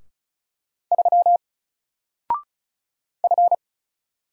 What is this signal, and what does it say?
Morse code, heard as a keyed tone, about 36 wpm, 700 Hz. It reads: 3 F
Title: Single Letter-Number Alt 35wpm